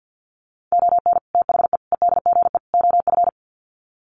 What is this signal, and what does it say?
Morse code, heard as a keyed tone, about 50 wpm, 700 Hz. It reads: ON THE EDGE OF
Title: Sets of 4 Words 50wpm